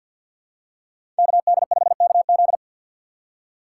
Morse code, heard as a keyed tone, about 51 wpm, 700 Hz. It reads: KD5KZ
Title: USA Call Signs - 5 Characters - 6x Character Spacing - 50wpm